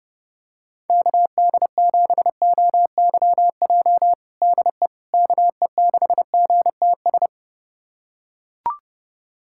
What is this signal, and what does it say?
Morse code, heard as a keyed tone, about 32 wpm, 700 Hz. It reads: KD7OYJ DE KE6GTS
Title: USA Callsign DE Callsign - 6 Characters - 30wpm